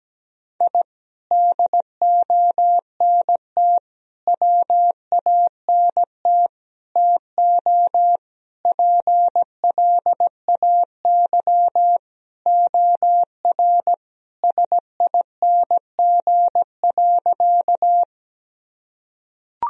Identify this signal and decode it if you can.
Morse code, heard as a keyed tone, about 17 wpm, 700 Hz. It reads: I DONT WANT TO PLAY OR SING.
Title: Sentences from Top 1000 Words - Review Part 2 - 17wpm